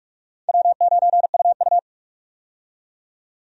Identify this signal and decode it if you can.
Morse code, heard as a keyed tone, about 45 wpm, 700 Hz. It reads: W9UU
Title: USA Call Signs - 4 Characters - 45wpm